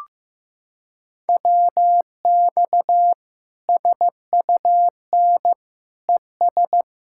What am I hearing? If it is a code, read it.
Morse code, heard as a keyed tone, about 16 wpm, 700 Hz. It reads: WX SUN ES
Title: Practice QSO - Exchange 2 - Info - 15wpm